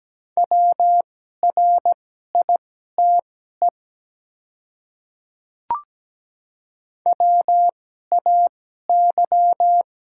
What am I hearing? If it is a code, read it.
Morse code, heard as a keyed tone, about 17 wpm, spaced as for 12 wpm, 700 Hz. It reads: WRITE WAY
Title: 2x Character Spacing - Top 100 Words - 17wpm